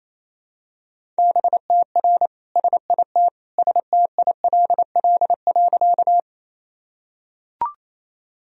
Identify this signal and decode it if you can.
Morse code, heard as a keyed tone, about 28 wpm, 700 Hz. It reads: BTR SIT STILL.
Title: Sentences from Top 500 Words and Abbreviations 28wpm